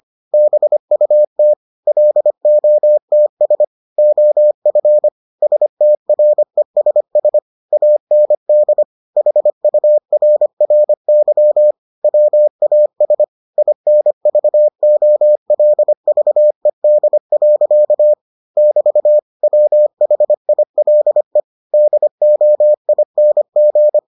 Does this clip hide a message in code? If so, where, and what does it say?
Yes: Morse code from 0.3 to 24.0 s, heard as a keyed tone, about 25 wpm, 600 Hz. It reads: BUT LOTS OF STRESS AND HURRY WAS INVOLVED. <BT> WHILE DOING